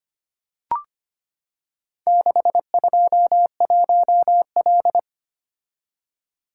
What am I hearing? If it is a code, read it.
Morse code, heard as a keyed tone, about 25 wpm, 700 Hz. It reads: 621L
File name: practice / 25wpm / Four Letter-Number Combinations - 25wpm